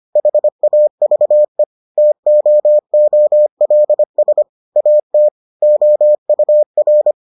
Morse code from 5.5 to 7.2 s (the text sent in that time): OUR